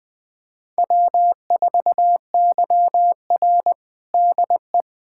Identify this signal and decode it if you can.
Morse code, heard as a keyed tone, about 20 wpm, 700 Hz. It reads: W4YR DE
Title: USA Callsign DE Callsign - 4 Characters - 20wpm